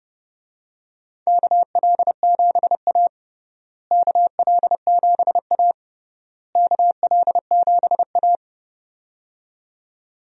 Morse code, heard as a keyed tone, about 29 wpm, 700 Hz. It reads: KL7A KL7A KL7A
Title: DXCC-100 Call Signs - 4 Characters - Repeated 3x - 30wpm